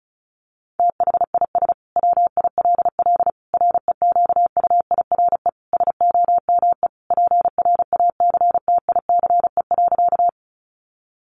Morse code, heard as a keyed tone, about 34 wpm, 700 Hz. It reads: THIS WILL REQUIRE SOME PRACTICE.
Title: Sentences from Top 900 Words 35wpm